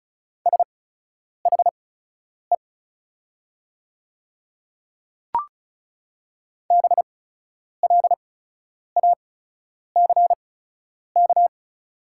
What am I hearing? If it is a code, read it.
Morse code, heard as a keyed tone, about 36 wpm, spaced as for 10 wpm, 690 Hz. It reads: SHE BLACK
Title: 8x Character Spacing - Top 500 Words - 35wpm